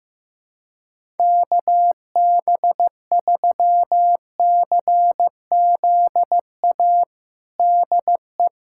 Morse code, heard as a keyed tone, about 15 wpm, 700 Hz. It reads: KB3CZA DE
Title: USA Callsign DE Callsign - 6 Characters - 15wpm